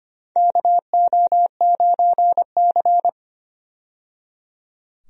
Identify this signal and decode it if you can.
Morse code, heard as a keyed tone, about 25 wpm, 700 Hz. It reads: KO9C
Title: USA Call Signs - 4 Characters - 25wpm